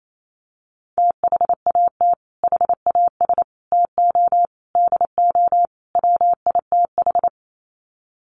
Morse code, heard as a keyed tone, about 28 wpm, 700 Hz. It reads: THAT HAS TO DO WITH
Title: Sets of 5 Words Encore2 28wpm